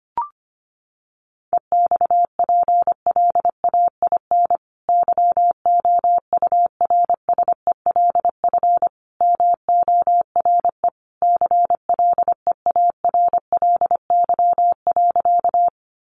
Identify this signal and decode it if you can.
Morse code, heard as a keyed tone, about 25 wpm, 700 Hz. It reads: EXPLAIN YOURSELF MORE CLEARLY.
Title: Sentences from Top 1300 Words 25wpm